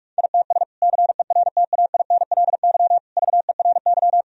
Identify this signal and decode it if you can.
Morse code, heard as a keyed tone, about 46 wpm, 700 Hz. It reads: ITS CERTAINLY VERY
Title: Sentences from Top 1000 Words - Review Part 2 - 45wpm